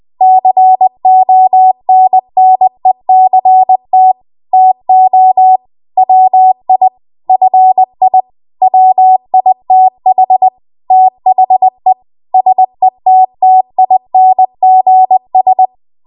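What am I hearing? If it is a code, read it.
Morse code, heard as a keyed tone, about 20 wpm, 750 Hz. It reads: CONNECT TO WI FI WITH THE SETTINGS
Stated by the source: Morse code practice at 20 WPM.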